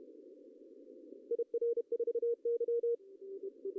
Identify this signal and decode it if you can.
Morse code, heard as a keyed tone, about 32 wpm, 470 Hz. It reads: IR4Y
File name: CW / cqww-cw-k5tr